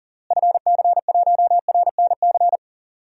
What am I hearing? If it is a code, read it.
Morse code, heard as a keyed tone, about 40 wpm, 700 Hz. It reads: FC1RNC
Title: DXCC-100 Call Signs - 6 Characters - Repeated 3x - 40wpm